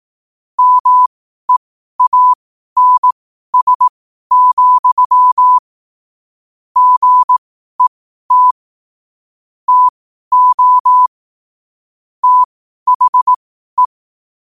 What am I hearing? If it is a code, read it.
Morse code, heard as a keyed tone, about 18 wpm, spaced as for 12 wpm, 1000 Hz. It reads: MEANS, GET TO THE
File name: qotdfiles / 12wpm